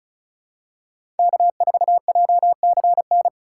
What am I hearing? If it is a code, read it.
Morse code, heard as a keyed tone, about 36 wpm, 690 Hz. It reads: K4JCN
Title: USA Call Signs - 5 Characters - 3x Character Spacing - 35wpm